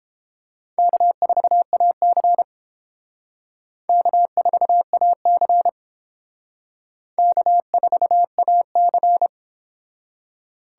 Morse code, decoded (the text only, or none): K4AC K4AC K4AC